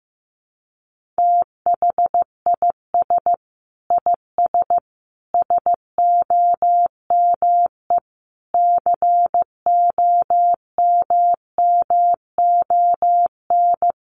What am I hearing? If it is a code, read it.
Morse code, heard as a keyed tone, about 15 wpm, 700 Hz. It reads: THIS IS SOME COMMON
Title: Sentences from Top 1400 Words 15wpm